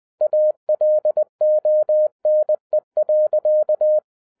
Morse code, heard as a keyed tone, about 20 wpm, 610 Hz. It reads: ALONE.
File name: 2012-01-mp3 / n05-CAME BACK TO HAM RADIO AFTER 30 YEARS OF INTERVAL-20wpm